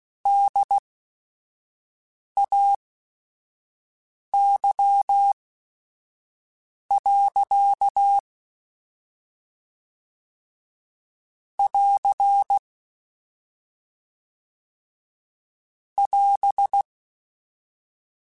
Morse code, heard as a keyed tone, about 16 wpm, spaced as for 5 wpm, 800 Hz. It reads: DAY. <AR> <AS>